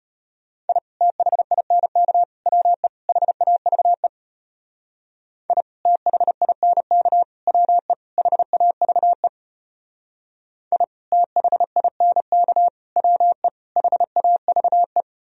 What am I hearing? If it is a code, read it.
Morse code, heard as a keyed tone, about 31 wpm, 700 Hz. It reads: I THINK WE HAVE I THINK WE HAVE I THINK WE HAVE